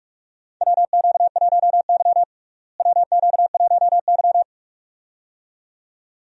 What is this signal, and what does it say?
Morse code, heard as a keyed tone, about 43 wpm, 700 Hz. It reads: WQ1Y WQ1Y
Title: USA Call Signs - 4 Characters - Repeated 2x - 45wpm